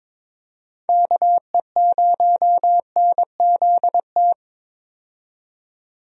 Morse code, heard as a keyed tone, about 22 wpm, 700 Hz. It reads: KE0NZT
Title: USA Call Signs - 6 Characters - 22wpm